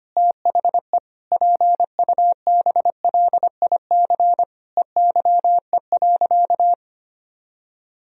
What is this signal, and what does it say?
Morse code, heard as a keyed tone, about 25 wpm, 700 Hz. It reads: THE PUBLIC EYE.
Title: Sentences from Top 1000 Words - Review Part 3 - 25wpm